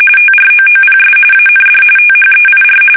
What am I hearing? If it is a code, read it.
RTTY, about 45.45 baud. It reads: TROOPS AND THEIR
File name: fsk850 / usnavy2